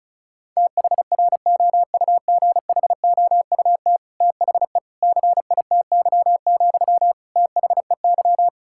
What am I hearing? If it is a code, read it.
Morse code, heard as a keyed tone, about 34 wpm, 700 Hz. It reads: THROUGHOUT THE CITY, THEY
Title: Sentences from Top 1000 Words - Review Part 5 - 35wpm